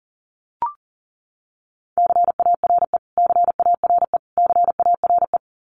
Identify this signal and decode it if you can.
Morse code, heard as a keyed tone, about 40 wpm, 700 Hz. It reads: CARE CARE CARE
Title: Top 300 Words Repeated 40wpm